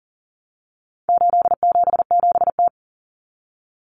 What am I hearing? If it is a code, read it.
Morse code, heard as a keyed tone, about 40 wpm, 700 Hz. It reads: 877T
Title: Common Cut Numbers - 4-Digit 40wpm